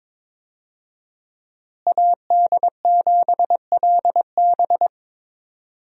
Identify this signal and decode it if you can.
Morse code, heard as a keyed tone, about 22 wpm, 700 Hz. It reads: AD7LB
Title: USA Call Signs - 5 Characters - Repeated 3x - 22wpm